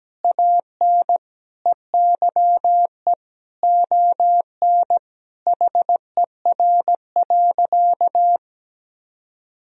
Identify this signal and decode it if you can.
Morse code, heard as a keyed tone, about 17 wpm, 700 Hz. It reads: AN EYE ON HER.
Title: Sentences from Top 1000 Words - Review Part 1 - 17wpm